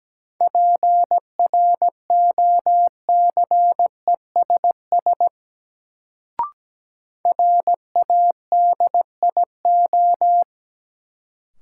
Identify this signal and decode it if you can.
Morse code, heard as a keyed tone, about 17 wpm, 700 Hz. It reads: PROCESS RADIO
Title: Sentences from Top 1000 Words - Review Part 3 - New Word Drill - 17wpm